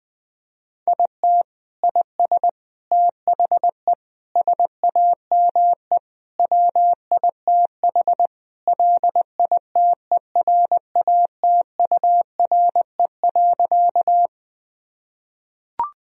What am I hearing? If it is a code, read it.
Morse code, heard as a keyed tone, about 21 wpm, 700 Hz. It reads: IT IS THE SAME WITH LITERATURE.
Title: Sentences from Top 1000 Words - Review Part 3 - 20wpm